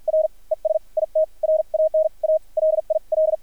Morse code, heard as a keyed tone, about 50 wpm, spaced as for 42 wpm, 630 Hz. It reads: R ES IT RAN A LIL
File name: archived / CRSnet